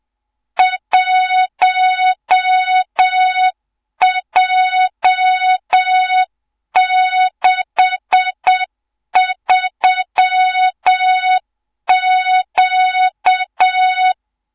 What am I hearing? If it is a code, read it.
Morse code, heard as a keyed tone, about 7 wpm, 3700 Hz, with noise. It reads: 1J63Q
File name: Morze-HA2MN / hang / 24 GYAKORLAT